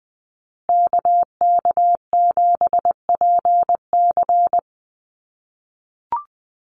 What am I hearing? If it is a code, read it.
Morse code, heard as a keyed tone, about 20 wpm, 700 Hz. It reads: KK7PC
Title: USA Call Signs Repeated 3x 20wpm